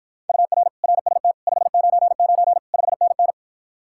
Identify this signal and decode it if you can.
Morse code, heard as a keyed tone, about 54 wpm, 700 Hz. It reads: UR RST 599 5NN